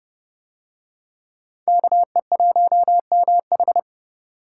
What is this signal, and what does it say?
Morse code, heard as a keyed tone, about 28 wpm, 700 Hz. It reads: KE1MH